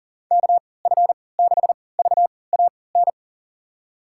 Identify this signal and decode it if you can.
Morse code, heard as a keyed tone, about 38 wpm, spaced as for 22 wpm, 700 Hz. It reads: KF6VAN
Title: USA Call Signs - 6 Characters - 3x Character Spacing - 40wpm